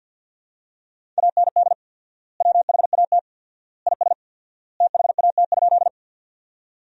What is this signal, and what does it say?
Morse code, heard as a keyed tone, about 50 wpm, 700 Hz. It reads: AND WHAT IS THAT?
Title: 4x Word-Spacing - Sentences from Top 100 Words - 50wpm